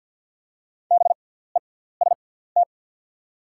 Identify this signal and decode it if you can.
Morse code, heard as a keyed tone, about 48 wpm, spaced as for 17 wpm, 690 Hz. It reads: BEST
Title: 6x Character Spacing - Top 500 Words - 50wpm